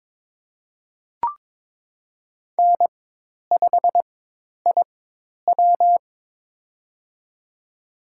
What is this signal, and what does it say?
Morse code, heard as a keyed tone, about 22 wpm, spaced as for 10 wpm, 700 Hz. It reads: N5IW
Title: USA Call Signs - 4 Characters - 4x Character Spacing - 22wpm